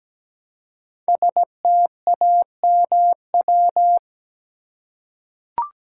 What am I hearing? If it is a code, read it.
Morse code, heard as a keyed tone, about 17 wpm, 700 Hz. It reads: STAMW
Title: Code Groups - 5 Letters - 17wpm